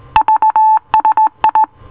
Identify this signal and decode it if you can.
Morse code, heard as a keyed tone, about 18 wpm, 870 Hz. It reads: VSI